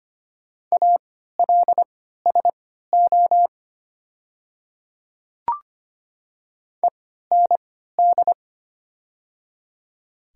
Morse code, heard as a keyed tone, about 25 wpm, spaced as for 14 wpm, 700 Hz. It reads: ALSO END
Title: 3x Character Spacing - Top 200 Words - 25wpm